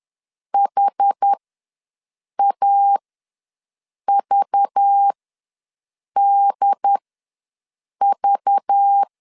Morse code, heard as a keyed tone, about 10 wpm, spaced as for 6 wpm, 800 Hz. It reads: HAVDV
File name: LEZIONI / 5 LBVC / 5 LBV ws 10wpm 3645